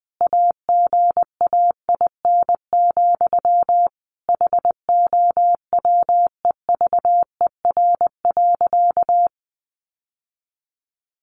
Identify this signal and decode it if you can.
Morse code, heard as a keyed tone, about 20 wpm, 700 Hz. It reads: AGAIN, HOWEVER.
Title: Sentences from Top 600 Words 20wpm